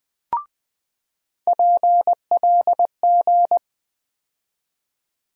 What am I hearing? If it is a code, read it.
Morse code, heard as a keyed tone, about 20 wpm, 700 Hz. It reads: PLG
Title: USA Call Sign Postfixes - 3 Letters 20wpm